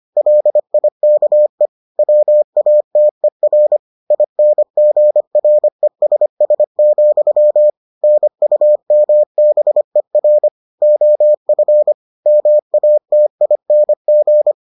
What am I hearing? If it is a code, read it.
Morse code, heard as a keyed tone, about 25 wpm, 600 Hz. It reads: LIKE WATER INGRESS, NUMBER OF MATING